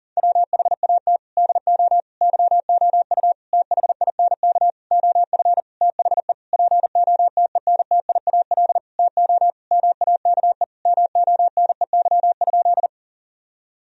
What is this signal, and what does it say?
Morse code, heard as a keyed tone, about 40 wpm, 700 Hz. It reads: WHAT DO YOU THINK OF THE POTENTIAL TO MAKE MONEY?
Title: Sentences from Top 1000 Words - Review Part 4 - 40wpm